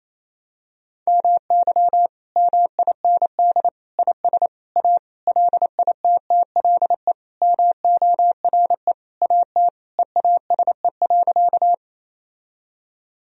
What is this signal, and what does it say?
Morse code, heard as a keyed tone, about 28 wpm, 700 Hz. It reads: MY MIND IS A LITTLE MORE AT EASE.
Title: Sentences from Top 1000 Words - Review Part 1 - 28wpm